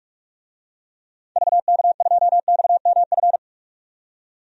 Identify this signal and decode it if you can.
Morse code, heard as a keyed tone, about 47 wpm, 700 Hz. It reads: VK2XMF
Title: DXCC-100 Call Signs - 6 Characters - 45wpm